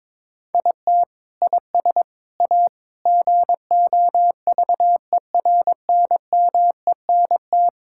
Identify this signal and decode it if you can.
Morse code, heard as a keyed tone, about 22 wpm, 700 Hz. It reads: IT IS A GOVERNMENT
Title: Sentences from Top 1000 Words - Review Part 5 - 22wpm